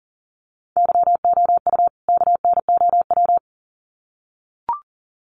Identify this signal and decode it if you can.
Morse code, heard as a keyed tone, about 40 wpm, 700 Hz. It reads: YOU KNOW
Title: Sets of 2 Words 40wpm